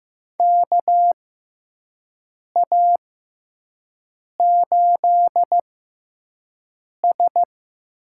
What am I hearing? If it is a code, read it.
Morse code, heard as a keyed tone, about 15 wpm, spaced as for 5 wpm, 700 Hz. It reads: KA8S
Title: USA Call Signs - 6 Characters - 6x Character Spacing - 15wpm